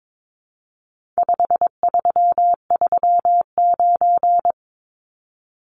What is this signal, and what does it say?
Morse code, heard as a keyed tone, about 22 wpm, 700 Hz. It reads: 5339
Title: Numbers - 4-Digit 22wpm